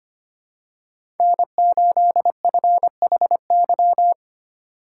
Morse code, heard as a keyed tone, about 25 wpm, 700 Hz. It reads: N8FHY
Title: USA Call Signs - 5 Characters - 2x Character Spacing - 25wpm